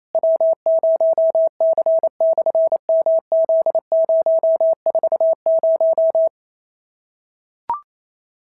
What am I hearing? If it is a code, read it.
Morse code, heard as a keyed tone, about 28 wpm, 690 Hz. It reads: W0C/MZ040
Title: SOTA QSO Practice 28wpm